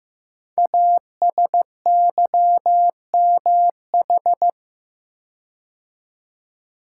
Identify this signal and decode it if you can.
Morse code, heard as a keyed tone, about 15 wpm, 700 Hz. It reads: ASYMH
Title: Code Groups - 5 Letters - 15wpm